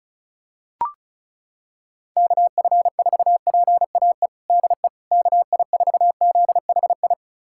Morse code, heard as a keyed tone, about 36 wpm, 700 Hz. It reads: KF4PAE DE KI4ZHI
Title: USA Callsign DE Callsign - 6 Characters - 35wpm